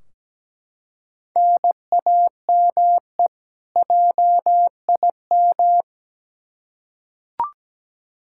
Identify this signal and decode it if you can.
Morse code, heard as a keyed tone, about 17 wpm, 700 Hz. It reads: NAME JIM
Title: K1USN SST QSO Practice 17wpm